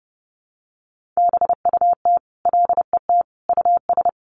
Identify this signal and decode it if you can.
Morse code, heard as a keyed tone, about 30 wpm, 700 Hz. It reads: BUT LET US
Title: Sentences from Top 1500 Words 30wpm